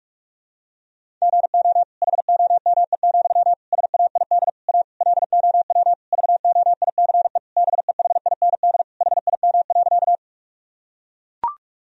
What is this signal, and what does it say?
Morse code, heard as a keyed tone, about 41 wpm, 700 Hz. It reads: GO HOME, SAID A LOW VOICE BEHIND HIM.
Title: Sentences from Top 1000 Words - Review Part 1 - 45wpm